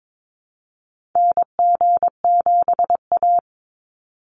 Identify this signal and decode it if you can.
Morse code, heard as a keyed tone, about 22 wpm, 700 Hz. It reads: NG7A
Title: NA SOTA Activator Callsigns 22wpm